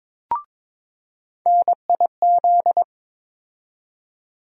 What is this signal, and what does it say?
Morse code, heard as a keyed tone, about 22 wpm, 700 Hz. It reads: NIZ
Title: USA Call Sign Postfixes - 3 Letters 22wpm